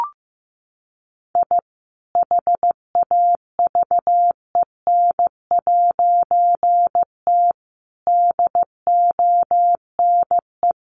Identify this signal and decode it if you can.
Morse code, heard as a keyed tone, about 15 wpm, 700 Hz. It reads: I HAVEN'T DONE
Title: A Sound Of Thunder 15wpm